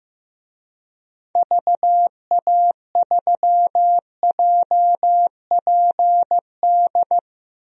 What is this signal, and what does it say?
Morse code, heard as a keyed tone, about 15 wpm, 700 Hz. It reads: VA3JPD
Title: DXCC-100 Call Signs - 6 Characters - 15wpm